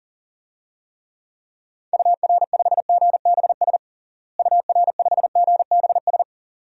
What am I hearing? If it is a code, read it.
Morse code, heard as a keyed tone, about 40 wpm, 700 Hz. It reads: UR5GBS UR5GBS
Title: DXCC-100 Call Signs - 6 Characters - Repeated 2x - 40wpm